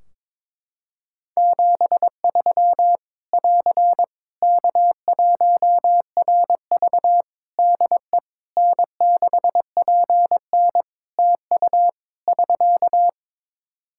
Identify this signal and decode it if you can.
Morse code, heard as a keyed tone, about 22 wpm, 700 Hz. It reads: 73 <AR> K1RV DE N6PN TU <SK>
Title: Practice QSO - Exchange 4 - Ending - 22wpm